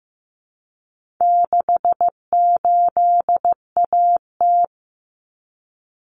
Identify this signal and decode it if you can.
Morse code, heard as a keyed tone, about 15 wpm, 700 Hz. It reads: 68AT
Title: Common Cut Numbers - 4-Digit 15wpm